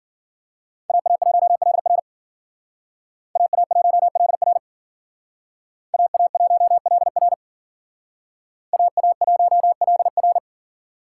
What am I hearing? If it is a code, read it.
Morse code, heard as a keyed tone, about 51 wpm, 700 Hz. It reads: AA1LR AA1LR AA1LR AA1LR